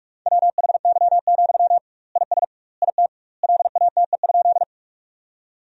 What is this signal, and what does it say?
Morse code, heard as a keyed tone, about 46 wpm, 700 Hz. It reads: WHY, IS IT LATE?
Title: 2x Word-Spacing - Sentences from Top 300 Words - 45wpm